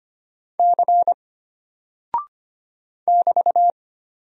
Morse code, heard as a keyed tone, about 25 wpm, 700 Hz. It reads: C <BT>
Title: Single Character (Letter:Number:Punctuation:Prosign) Alt 25wpm